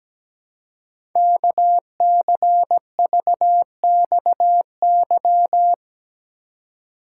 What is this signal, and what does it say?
Morse code, heard as a keyed tone, about 17 wpm, 700 Hz. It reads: KCVXY
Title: Code Groups - 5 Letters - 17wpm